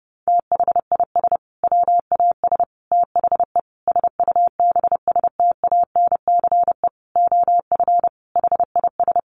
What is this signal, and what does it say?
Morse code, heard as a keyed tone, about 30 wpm, 700 Hz. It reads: THIS WAS THE SUBSTANCE OF HIS
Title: Sentences from Top 1500 Words 30wpm